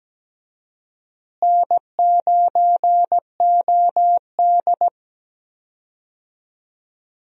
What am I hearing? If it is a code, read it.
Morse code, heard as a keyed tone, about 17 wpm, 700 Hz. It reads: N9OD
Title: USA Call Signs - 4 Characters - Repeated 2x - 17wpm